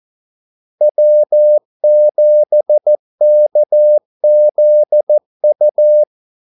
Morse code, heard as a keyed tone, about 14 wpm, 600 Hz. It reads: W7KZU
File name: Callsigns-14wpm